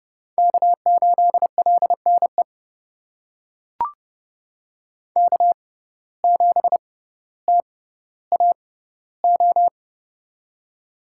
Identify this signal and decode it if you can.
Morse code, heard as a keyed tone, about 30 wpm, spaced as for 15 wpm, 700 Hz. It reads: K8LNE K7TAO
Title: USA Call Signs - 5 Characters - 6x Character Spacing - 30wpm